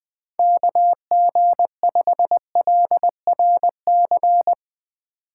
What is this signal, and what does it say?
Morse code, heard as a keyed tone, about 20 wpm, 700 Hz. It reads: KG5LRC
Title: USA Call Signs - 6 Characters - 4x Character Spacing - 20wpm